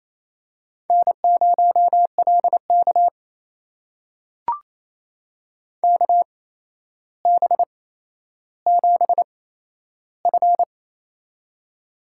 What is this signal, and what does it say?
Morse code, heard as a keyed tone, about 28 wpm, spaced as for 12 wpm, 700 Hz. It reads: N0LK KB7F
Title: USA Call Signs - 4 Characters - 8x Character Spacing - 28wpm